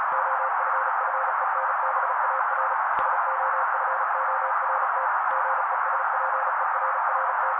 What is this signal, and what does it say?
Morse code, heard as a keyed tone, about 36 wpm, 530 Hz. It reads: OFFENBAR SOFORT GESPERRT
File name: daily / 35wpm noise